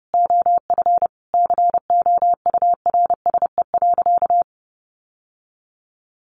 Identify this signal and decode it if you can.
Morse code, heard as a keyed tone, about 30 wpm, 700 Hz. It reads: OF COURSE.
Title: Sentences from Top 1100 Words 30wpm